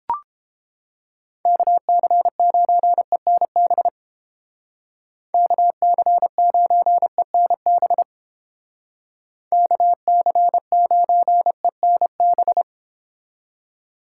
Morse code, heard as a keyed tone, about 29 wpm, 700 Hz. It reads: KC9ENB KC9ENB KC9ENB